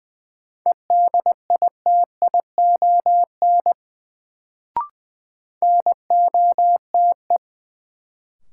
Morse code, heard as a keyed tone, about 20 wpm, 700 Hz. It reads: EDITION NOTE
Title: Sets of 5 Words - New Word Drill - 20wpm